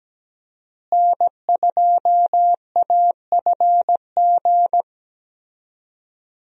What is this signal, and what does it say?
Morse code, heard as a keyed tone, about 17 wpm, 700 Hz. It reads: N2AFG
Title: USA Call Signs - 5 Characters - Repeated 3x - 17wpm